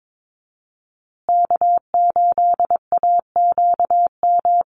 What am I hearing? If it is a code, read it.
Morse code, heard as a keyed tone, about 22 wpm, 700 Hz. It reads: K8AQM
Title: Call Signs Repeated 3x - 22wpm